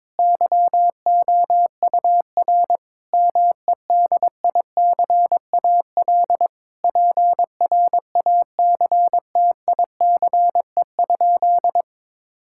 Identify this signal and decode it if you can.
Morse code, heard as a keyed tone, about 22 wpm, 700 Hz. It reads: YOUR MEDICAL PRACTICE?
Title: Sentences from Top 1000 Words - Review Part 5 - 22wpm